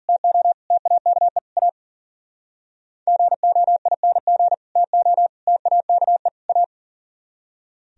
Morse code, heard as a keyed tone, about 52 wpm, 700 Hz. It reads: TO TAKE A GOING TO TAKE A